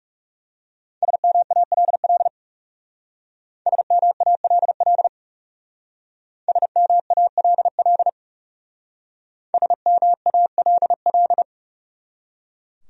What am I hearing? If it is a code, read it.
Morse code, heard as a keyed tone, about 37 wpm, 700 Hz. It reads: SMALL SMALL SMALL SMALL
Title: Speed-Racing Top 300 Words 30wpm